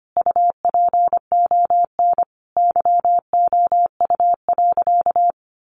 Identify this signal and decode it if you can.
Morse code, heard as a keyed tone, about 25 wpm, 700 Hz. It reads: UPON YOU.
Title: Sentences from Top 1500 Words 25wpm